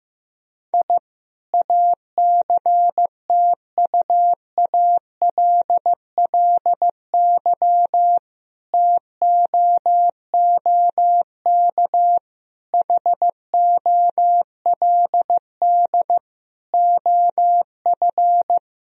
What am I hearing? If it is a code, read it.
Morse code, heard as a keyed tone, about 15 wpm, 700 Hz. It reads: I ACTUALLY TOOK HOLD OF
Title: Sentences from Top 1000 Words - Review Part 4 - 15wpm